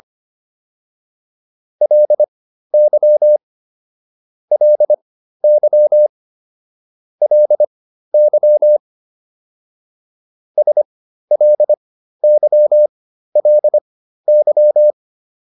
Morse code, heard as a keyed tone, about 25 wpm, spaced as for 13 wpm, 600 Hz. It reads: LY LY LY SLYLY
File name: LY5 13